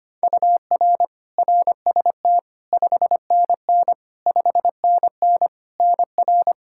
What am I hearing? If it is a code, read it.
Morse code, heard as a keyed tone, about 25 wpm, 700 Hz. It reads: UR RST 5NN 5NN NR
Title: K1USN SST QSO Practice 25wpm